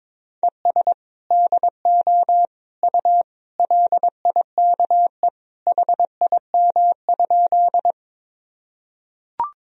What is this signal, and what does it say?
Morse code, heard as a keyed tone, about 22 wpm, 700 Hz. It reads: ES DO U LIKE HIM?
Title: Sentences from Top 100 Words and Abbreviations 22wpm